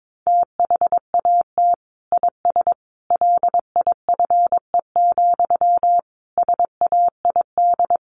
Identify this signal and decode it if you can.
Morse code, heard as a keyed tone, about 22 wpm, 700 Hz. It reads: THAT IS LIFE, SAID
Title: Sentences from Top 900 Words 22wpm